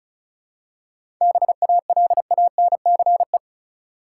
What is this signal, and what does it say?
Morse code, heard as a keyed tone, about 35 wpm, 700 Hz. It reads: BALANCE
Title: Seven Letter Words - 35wpm